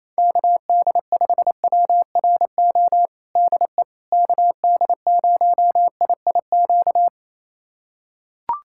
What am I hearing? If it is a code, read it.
Morse code, heard as a keyed tone, about 28 wpm, 700 Hz. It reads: KD5WRO DE KD0IIQ
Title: USA Callsign DE Callsign - 6 Characters - 28wpm